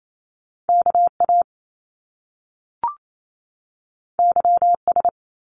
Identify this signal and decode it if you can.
Morse code, heard as a keyed tone, about 28 wpm, 700 Hz. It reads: KA YS
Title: Top Two Letter Combinations in Words 28wpm